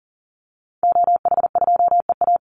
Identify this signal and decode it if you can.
Morse code, heard as a keyed tone, about 37 wpm, 700 Hz. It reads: OH2EA
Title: Call Signs Repeated 3x - 40wpm